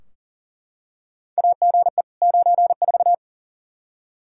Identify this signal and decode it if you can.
Morse code, heard as a keyed tone, about 40 wpm, 700 Hz. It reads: AGE 94
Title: Practice QSO - Exchange 3 - More Info - 40wpm